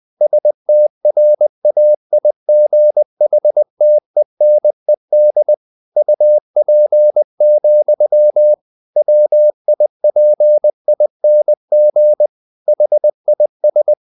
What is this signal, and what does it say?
Morse code, heard as a keyed tone, about 20 wpm, 600 Hz. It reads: STRAIGHTENED UP, WIPING HIS
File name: the old man and the sea / the old man and the sea part 1 20wpm